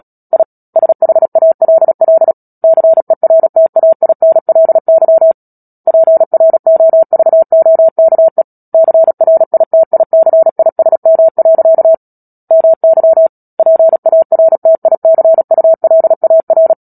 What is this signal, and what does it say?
Morse code, heard as a keyed tone, about 37 wpm, 650 Hz. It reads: I SHALL CERTAINLY PROVOKE CRITICISM. MY PARTICULAR
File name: Book Two / 10-Chapter 10 - Epilogue - 36 WPM